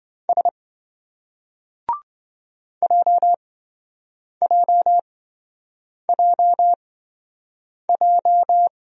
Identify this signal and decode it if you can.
Morse code, heard as a keyed tone, about 26 wpm, spaced as for 7 wpm, 700 Hz. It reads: S JJJJ